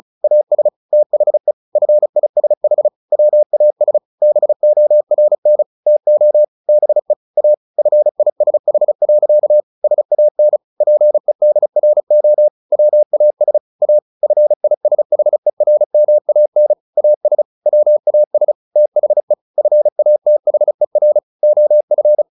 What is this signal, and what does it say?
Morse code, heard as a keyed tone, about 34 wpm, 600 Hz. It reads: AS THE FISH WAS BORN TO BE A FISH. SAN PEDRO WAS A FISHERMAN AS WAS THE FATHER OF
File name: the old man and the sea 2 35wpm